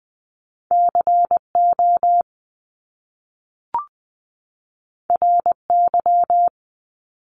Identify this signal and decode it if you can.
Morse code, heard as a keyed tone, about 20 wpm, 700 Hz. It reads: CO RY
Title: Top Two Letter Combinations in Words 20wpm